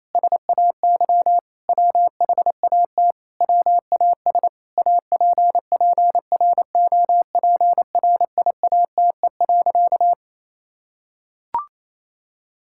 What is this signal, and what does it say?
Morse code, heard as a keyed tone, about 28 wpm, 700 Hz. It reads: SAY WHAT WAS APPROPRIATE.
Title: Sentences from Top 1000 Words - Review Part 5 - 28wpm